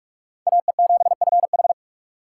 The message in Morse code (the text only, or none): AE7FH